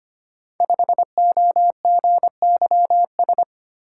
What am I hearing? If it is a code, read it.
Morse code, heard as a keyed tone, about 25 wpm, 700 Hz. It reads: 5OGYS
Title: Code Groups - 5 Letter:Numbers - 25wpm